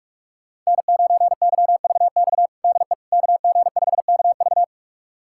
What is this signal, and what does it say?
Morse code, heard as a keyed tone, about 41 wpm, 700 Hz. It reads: N9YVX DE KG5KV
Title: USA Callsign DE Callsign - 5 Characters - 45wpm